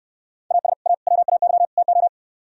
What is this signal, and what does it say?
Morse code, heard as a keyed tone, about 68 wpm, 700 Hz. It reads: AS A WAY TO